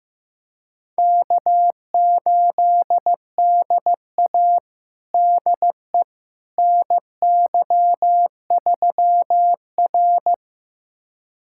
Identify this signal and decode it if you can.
Morse code, heard as a keyed tone, about 15 wpm, 700 Hz. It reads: K8DA DE NY3R
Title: USA Callsign DE Callsign - 4 Characters - 15wpm